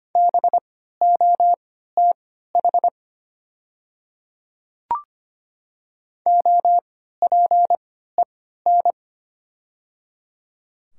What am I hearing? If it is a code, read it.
Morse code, heard as a keyed tone, about 25 wpm, spaced as for 14 wpm, 690 Hz. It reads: BOTH OPEN
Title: 3x Character Spacing - Top 500 Words - 25wpm